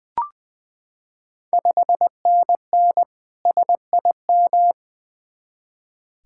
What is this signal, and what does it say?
Morse code, heard as a keyed tone, about 20 wpm, 700 Hz. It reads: 5NN SIM
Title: Ontario QSO Party Exchange 20wpm